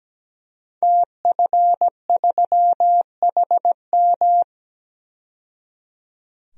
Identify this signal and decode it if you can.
Morse code, heard as a keyed tone, about 17 wpm, 700 Hz. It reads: TF3HM
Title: DXCC-100 Call Signs - 5 Characters - 17wpm